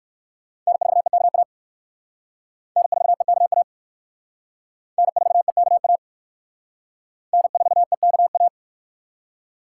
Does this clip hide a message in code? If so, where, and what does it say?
Yes: Morse code from 0.7 to 8.5 s, heard as a keyed tone, about 67 wpm, 700 Hz. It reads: N4EKA N4EKA N4EKA N4EKA